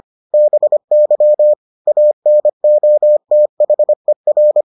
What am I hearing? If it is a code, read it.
Morse code, heard as a keyed tone, about 25 wpm, 600 Hz. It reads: BY ANOTHER